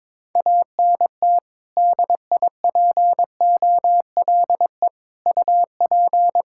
Morse code, heard as a keyed tone, about 22 wpm, 700 Hz. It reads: ANT DIPOLE UP
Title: Practice QSO - Exchange 2 - Info - 22wpm